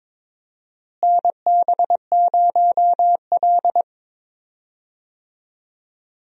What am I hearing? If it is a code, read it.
Morse code, heard as a keyed tone, about 22 wpm, 700 Hz. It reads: NB0L
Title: USA Call Signs - 4 Characters - Repeated 2x - 22wpm